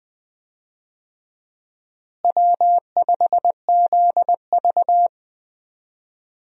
Morse code, heard as a keyed tone, about 20 wpm, 700 Hz. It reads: W5ZV
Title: USA Call Signs - 4 Characters - Repeated 2x - 20wpm